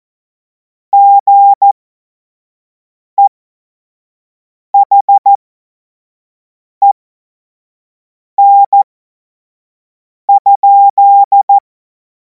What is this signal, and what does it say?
Morse code, heard as a keyed tone, about 14 wpm, spaced as for 5 wpm, 800 Hz. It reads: GEHEN?
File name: Morsekurs / Witze / Witz 24. März 2026 25 F70